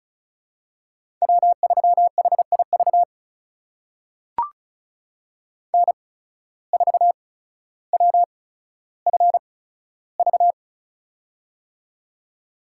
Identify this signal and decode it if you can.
Morse code, heard as a keyed tone, about 36 wpm, spaced as for 15 wpm, 700 Hz. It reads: W3HIV N4WFV
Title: USA Call Signs - 5 Characters - 8x Character Spacing - 35wpm